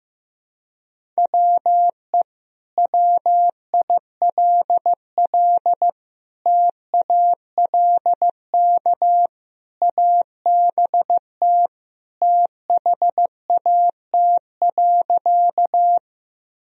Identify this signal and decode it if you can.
Morse code, heard as a keyed tone, about 16 wpm, 700 Hz. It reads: WE WILL TALK ABT THAT.
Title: Sentences from Top 400 Words and Abbreviations 15wpm